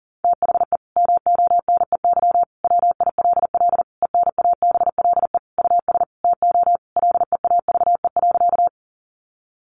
Morse code, heard as a keyed tone, about 40 wpm, 700 Hz. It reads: THE MONEY WILL ENABLE US TO LEAVE.
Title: Sentences from Top 1500 Words 40wpm